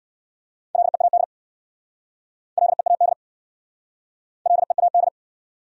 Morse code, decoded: LEAD LEAD LEAD

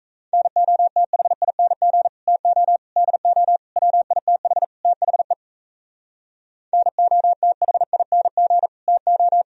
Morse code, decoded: NOTHING TO DO WITH THE NOTHING TO

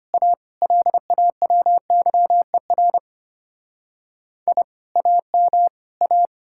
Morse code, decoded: A LAWYER I AM A

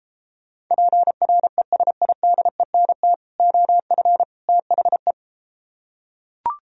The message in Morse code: PRESIDENT OF THE

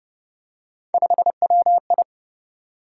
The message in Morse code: 5WI

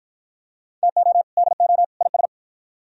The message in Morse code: TO DO IS

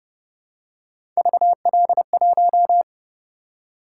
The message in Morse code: VL1

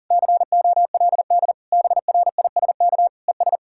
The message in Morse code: COLD BRISK ES

